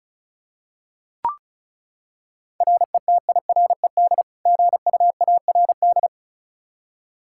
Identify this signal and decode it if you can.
Morse code, heard as a keyed tone, about 34 wpm, 700 Hz. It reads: RETIRED GUARD
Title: Practice QSO - Exchange 3 - More Info - 35wpm